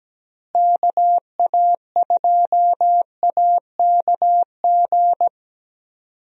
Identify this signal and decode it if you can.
Morse code, heard as a keyed tone, about 17 wpm, 700 Hz. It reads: KA2AKG
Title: USA Call Signs - 6 Characters - Repeated 3x - 17wpm